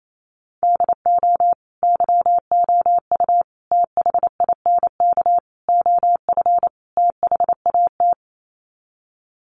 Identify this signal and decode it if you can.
Morse code, heard as a keyed tone, about 28 wpm, 700 Hz. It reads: DO YOU THINK OF THAT
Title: Sets of 5 Words Encore2 28wpm